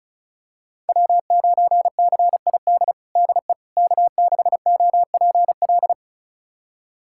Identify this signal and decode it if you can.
Morse code, heard as a keyed tone, about 34 wpm, 700 Hz. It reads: W9CID DE K6OPL
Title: USA Callsign DE Callsign - 5 Characters - 35wpm